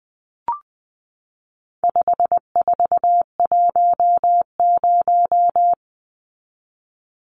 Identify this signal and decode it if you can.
Morse code, heard as a keyed tone, about 20 wpm, 700 Hz. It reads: 5410
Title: Numbers - 4-Digit 20wpm